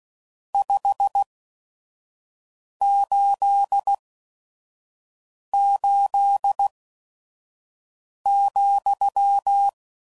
Morse code, heard as a keyed tone, about 16 wpm, spaced as for 5 wpm, 800 Hz. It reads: 588,
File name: CW Code Groups @5WPM / QSOs / QSO WA7QOX 1